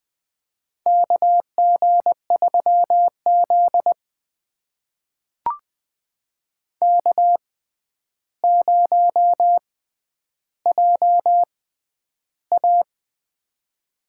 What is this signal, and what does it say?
Morse code, heard as a keyed tone, about 20 wpm, spaced as for 10 wpm, 700 Hz. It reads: KG3Z K0JA
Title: USA Call Signs - 4 Characters - 6x Character Spacing - 20wpm